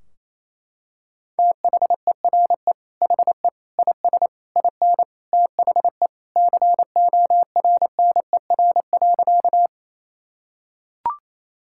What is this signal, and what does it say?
Morse code, heard as a keyed tone, about 28 wpm, 700 Hz. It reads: THERE HE IS IN THE CORNER.
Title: Sentences from Top 1600 Words 28wpm